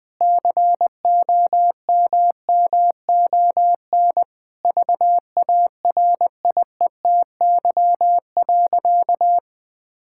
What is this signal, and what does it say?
Morse code, heard as a keyed tone, about 20 wpm, 700 Hz. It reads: COMMON VARIETY.
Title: Sentences with 6-Word Phrases - 20wpm